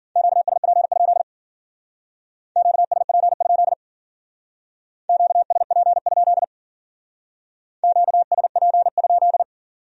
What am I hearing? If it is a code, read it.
Morse code, heard as a keyed tone, about 49 wpm, 700 Hz. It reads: QSP? QSP? QSP? QSP?
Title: Speed-Racing Q-Codes - 40wpm